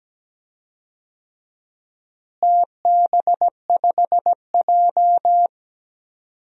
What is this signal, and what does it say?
Morse code, heard as a keyed tone, about 17 wpm, 700 Hz. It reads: TB5J
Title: DXCC-100 Call Signs - 4 Characters - Repeated 2x - 17wpm